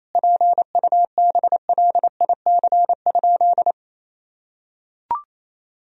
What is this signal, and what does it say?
Morse code, heard as a keyed tone, about 28 wpm, 700 Hz. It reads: PUBLIC?
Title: Sentences from Top 1000 Words - Review Part 3 - 28wpm